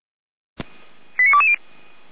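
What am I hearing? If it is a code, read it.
Five-tone selective call (ZVEI), heard as quick runs of five tones, tones of about 70 ms. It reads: 89220
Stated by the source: ZVEI five-tone selective call received with an SDR.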